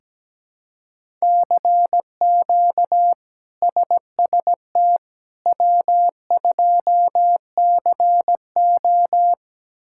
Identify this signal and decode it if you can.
Morse code, heard as a keyed tone, about 17 wpm, 700 Hz. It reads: CQ SST W2CO
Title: K1USN SST QSO Practice - 17wpm